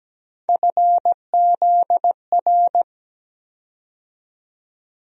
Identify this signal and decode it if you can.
Morse code, heard as a keyed tone, about 17 wpm, 700 Hz. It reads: FZR
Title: USA Call Sign Postfixes - 3 Letters 17wpm